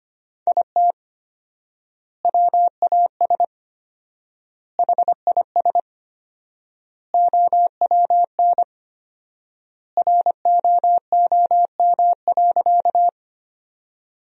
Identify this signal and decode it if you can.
Morse code, heard as a keyed tone, about 25 wpm, 700 Hz. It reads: IT WAS HIS OWN ROOM.
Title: 4x Word-Spacing - Sentences from Top 300 Words - 25wpm